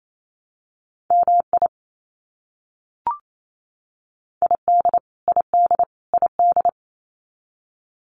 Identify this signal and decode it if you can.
Morse code, heard as a keyed tone, about 28 wpm, 700 Hz. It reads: MI ID ID ID
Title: US States Repeated 3x 28wpm